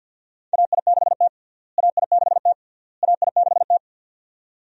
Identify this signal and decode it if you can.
Morse code, heard as a keyed tone, about 51 wpm, 700 Hz. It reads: AI6T AI6T AI6T
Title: USA Call Signs - 4 Characters - Repeated 3x - 50wpm